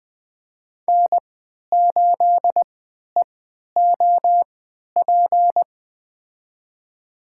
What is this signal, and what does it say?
Morse code, heard as a keyed tone, about 20 wpm, spaced as for 11 wpm, 700 Hz. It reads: N8EOP
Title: USA Call Signs - 3x Character Spacing - 20wpm